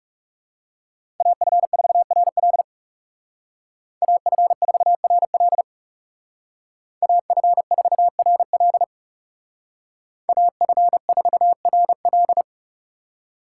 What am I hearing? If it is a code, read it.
Morse code, heard as a keyed tone, about 38 wpm, 700 Hz. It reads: AF4RL AF4RL AF4RL AF4RL
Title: Speed Racing - USA 2x2 Call Signs 30wpm